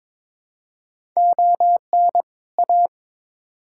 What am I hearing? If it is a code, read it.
Morse code, heard as a keyed tone, about 22 wpm, 700 Hz. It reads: ON A
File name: Build A Phrase - 2-Word Phrases - 22wpm